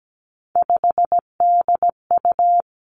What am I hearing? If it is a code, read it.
Morse code, heard as a keyed tone, about 17 wpm, 700 Hz. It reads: 5DU
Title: Three Challenging Letter-Number Combinations 17wpm